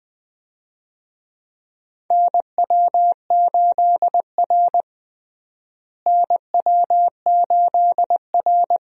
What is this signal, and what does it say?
Morse code, heard as a keyed tone, about 20 wpm, 700 Hz. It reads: NW8R NW8R
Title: USA Call Signs - 4 Characters - Repeated 2x - 20wpm